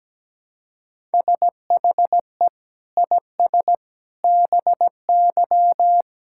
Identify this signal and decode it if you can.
Morse code, heard as a keyed tone, about 17 wpm, 700 Hz. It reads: SHE IS BY
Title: Sentences from Top 1600 Words 17wpm